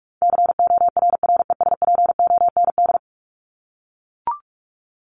Morse code, heard as a keyed tone, about 44 wpm, 700 Hz. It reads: CORRESPOND
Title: ING Suffix - Top 100 Words with Base Words 45wpm